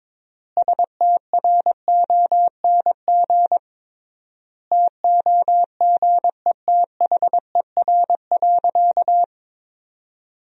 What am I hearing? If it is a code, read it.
Morse code, heard as a keyed tone, about 22 wpm, 700 Hz. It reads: STRONG TOGETHER.
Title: 3x Word-Spacing - Sentences from Top 500 Words - 22wpm